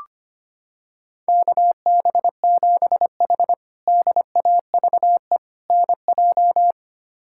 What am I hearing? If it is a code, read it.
Morse code, heard as a keyed tone, about 25 wpm, 700 Hz. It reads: KB7H DAVE NJ
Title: K1USN SST QSO Practice - 25wpm